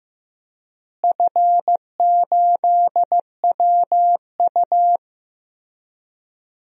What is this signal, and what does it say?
Morse code, heard as a keyed tone, about 15 wpm, 700 Hz. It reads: F8WU
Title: DXCC-100 Call Signs - 4 Characters - Repeated 3x - 15wpm